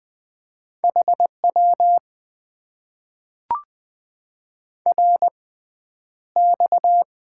Morse code, heard as a keyed tone, about 21 wpm, spaced as for 10 wpm, 700 Hz. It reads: HW RX
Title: Top 90 QSO Elements - 6x Character Spacing - 20wpm